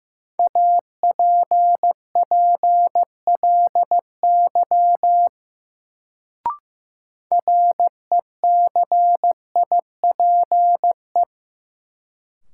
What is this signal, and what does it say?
Morse code, heard as a keyed tone, about 15 wpm, 700 Hz. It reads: APPLY RECIPE
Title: Sentences from Top 1200 Words - New Word Drill Alt - 15wpm